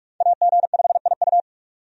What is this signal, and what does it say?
Morse code, heard as a keyed tone, about 46 wpm, 700 Hz. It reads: AG5IU
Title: USA Call Signs - 5 Characters - 45wpm